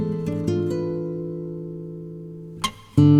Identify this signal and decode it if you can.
Dial tone (North America), 350 + 440 Hz.